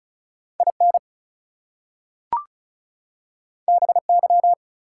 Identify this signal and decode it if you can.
Morse code, heard as a keyed tone, about 36 wpm, 700 Hz. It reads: IN BY
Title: Two Letter Words - 35wpm